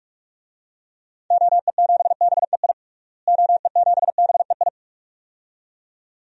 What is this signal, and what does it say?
Morse code, heard as a keyed tone, about 46 wpm, 700 Hz. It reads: OE7BEI OE7BEI
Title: DXCC-100 Call Signs - 6 Characters - Repeated 2x - 45wpm